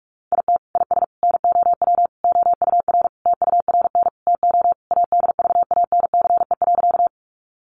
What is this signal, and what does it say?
Morse code, heard as a keyed tone, about 44 wpm, 700 Hz. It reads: IT IS NOW OUR TURN TO ADVANCE.
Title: Sentences from Top 1400 Words 45wpm